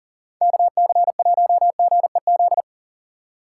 Morse code, heard as a keyed tone, about 39 wpm, 700 Hz. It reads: KC1GEZ
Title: USA Call Signs - 6 Characters - 4x Character Spacing - 40wpm